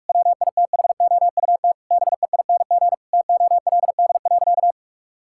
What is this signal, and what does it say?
Morse code, heard as a keyed tone, about 41 wpm, 700 Hz. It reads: WITHOUT BEING TOLD.
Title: Sentences from Top 1000 Words - Review Part 6 - 45wpm